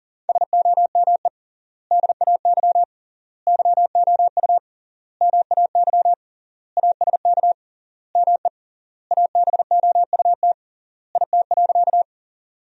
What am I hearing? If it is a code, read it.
Morse code, heard as a keyed tone, about 37 wpm, 700 Hz. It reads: SOME DAY YOU MAY ASK ME ABOUT IT.
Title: 3x Word-Spacing - Sentences from Top 200 Words - 40wpm